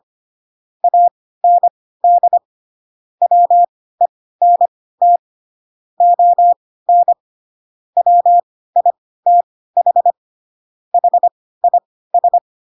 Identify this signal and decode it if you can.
Morse code, heard as a keyed tone, about 25 wpm, spaced as for 16 wpm, 700 Hz. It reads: AND WENT ON WITH HIS
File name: Pinocchio / Pinocchio ch3 10 25 ik1hge